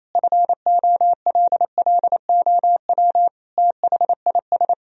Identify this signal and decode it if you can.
Morse code, heard as a keyed tone, about 28 wpm, 700 Hz. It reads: FOLLOW THIS
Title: Sentences from Top 1000 Words - Review Part 1 - 28wpm